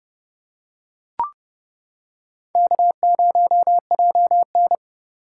Morse code, heard as a keyed tone, about 29 wpm, 700 Hz. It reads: K0JN